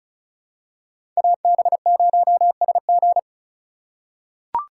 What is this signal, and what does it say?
Morse code, heard as a keyed tone, about 34 wpm, 700 Hz. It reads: AB0SG
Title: USA Call Signs - 5 Characters - 2x Character Spacing - 35wpm